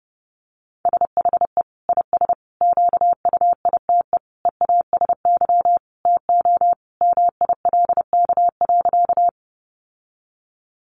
Morse code, heard as a keyed tone, about 30 wpm, 700 Hz. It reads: SHE IS QUITE EASY TO MILK.
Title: Sentences from Top 1500 Words 30wpm